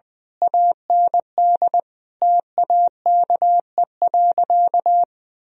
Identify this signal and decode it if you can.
Morse code, heard as a keyed tone, about 20 wpm, 700 Hz. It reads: AND TAKE.
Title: Sentences from Top 1000 Words - Review Part 4 - 20wpm